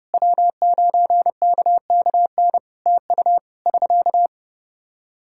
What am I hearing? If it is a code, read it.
Morse code, heard as a keyed tone, about 30 wpm, 700 Hz. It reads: W9KKN TU <SK>
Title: Practice QSO - Exchange 4 - Ending - 30wpm